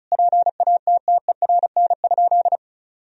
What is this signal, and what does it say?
Morse code, heard as a keyed tone, about 36 wpm, 700 Hz. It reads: PATTERN?
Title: Sentences from Top 1000 Words - Review Part 2 - 35wpm